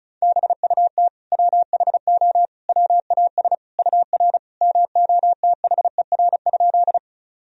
Morse code, heard as a keyed tone, about 36 wpm, 700 Hz. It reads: BUT WHO WAS UR MOTHER?
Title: Sentences from Top 200 Words and Abbreviations 35wpm